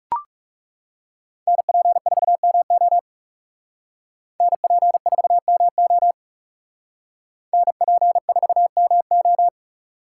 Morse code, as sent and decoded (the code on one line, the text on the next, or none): -. .--. ....- -- --- / -. .--. ....- -- --- / -. .--. ....- -- ---
NP4MO NP4MO NP4MO